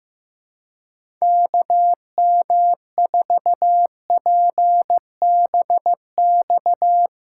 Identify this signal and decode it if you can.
Morse code, heard as a keyed tone, about 15 wpm, 700 Hz. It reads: KM4PBX
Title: USA Callsign DE Callsign - 6 Characters - 15wpm